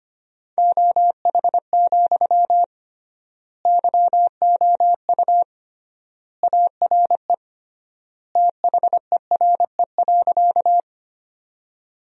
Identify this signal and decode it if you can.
Morse code, heard as a keyed tone, about 25 wpm, 700 Hz. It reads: OH, YOU ARE THERE.
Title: 3x Word-Spacing - Sentences from Top 500 Words - 25wpm